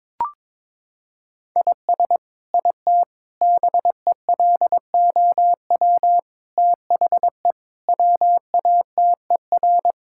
Morse code, heard as a keyed tone, about 21 wpm, 700 Hz. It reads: IS IT BELOW THE WATER
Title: Sentences from Top 1000 Words - Review Part 5 - 22wpm